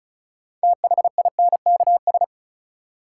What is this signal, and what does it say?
Morse code, heard as a keyed tone, about 36 wpm, 700 Hz. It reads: THINKS
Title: Sentences from Top 1000 Words - Review Part 6 - New Word Drill - 35wpm